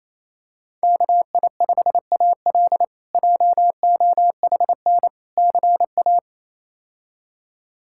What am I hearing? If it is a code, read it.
Morse code, heard as a keyed tone, about 28 wpm, 700 Hz. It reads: KI5AL JOHN CA
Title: K1USN SST QSO Practice - 28wpm